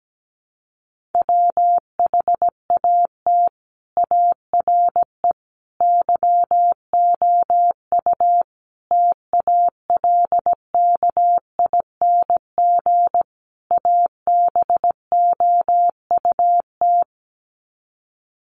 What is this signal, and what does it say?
Morse code, heard as a keyed tone, about 17 wpm, 700 Hz. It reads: WHAT ARE YOU TALKING ABOUT
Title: Sets of 5 Words 17wpm